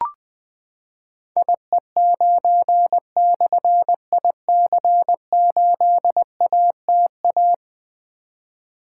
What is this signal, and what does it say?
Morse code, heard as a keyed tone, about 20 wpm, 700 Hz. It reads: IE9/IC8ATA
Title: Difficult Call Signs 20wpm